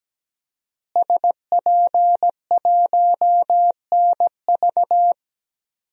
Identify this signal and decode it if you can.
Morse code, heard as a keyed tone, about 17 wpm, 700 Hz. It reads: SP1NV
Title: DXCC-100 Call Signs - 5 Characters - Repeated 3x - 17wpm